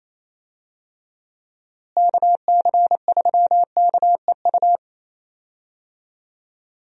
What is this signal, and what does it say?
Morse code, heard as a keyed tone, about 28 wpm, 700 Hz. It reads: KC3KEU
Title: USA Call Signs - 6 Characters - Repeated 3x - 28wpm